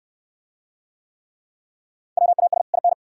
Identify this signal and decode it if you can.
Morse code, heard as a keyed tone, about 60 wpm, 700 Hz. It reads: WAS IN